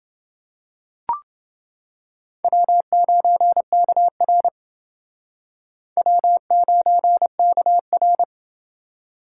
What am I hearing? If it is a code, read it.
Morse code, heard as a keyed tone, about 28 wpm, 690 Hz. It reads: W9KR W9KR